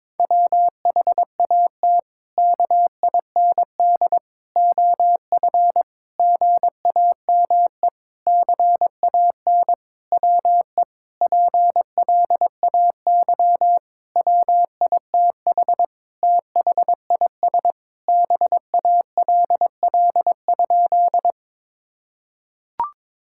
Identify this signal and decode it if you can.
Morse code, heard as a keyed tone, about 22 wpm, 700 Hz. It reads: WHAT KIND OF GAME CAN WE PLAY WITH THIS BALL?
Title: Sentences from Top 1000 Words - Review Part 6 - 22wpm